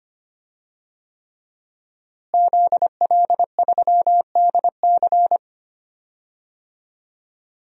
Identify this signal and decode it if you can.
Morse code, heard as a keyed tone, about 25 wpm, 700 Hz. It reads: ZL3DC
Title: DXCC-100 Call Signs - 5 Characters - Repeated 3x - 25wpm